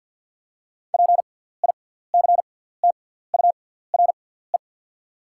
Morse code, heard as a keyed tone, about 48 wpm, spaced as for 17 wpm, 690 Hz. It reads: PICTURE
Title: 6x Character Spacing - Top 200 Words - 50wpm